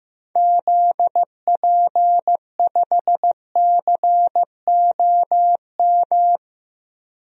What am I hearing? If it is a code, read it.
Morse code, heard as a keyed tone, about 15 wpm, 700 Hz. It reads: ZP5COM
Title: DXCC-100 Call Signs - 6 Characters - 15wpm